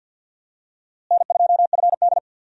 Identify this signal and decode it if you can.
Morse code, heard as a keyed tone, about 49 wpm, 700 Hz. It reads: N2FD
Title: USA Call Signs - 4 Characters - Repeated 3x - 50wpm